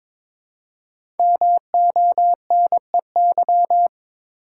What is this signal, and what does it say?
Morse code, heard as a keyed tone, about 22 wpm, 700 Hz. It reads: MONEY
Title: Five Letter Words - 22wpm